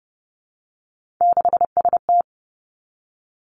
Morse code, heard as a keyed tone, about 30 wpm, 700 Hz. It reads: 6ST
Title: Three Challenging Letter-Number Combinations 30wpm